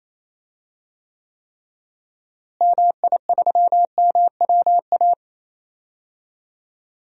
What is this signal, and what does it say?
Morse code, heard as a keyed tone, about 28 wpm, 700 Hz. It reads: MI3MWA
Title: DXCC-100 Call Signs - 6 Characters - Repeated 3x - 28wpm